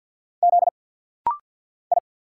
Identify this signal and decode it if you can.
Morse code, heard as a keyed tone, about 49 wpm, 700 Hz. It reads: Z I
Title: Single Letter-Number Alt - Rapid-Fire 50wpm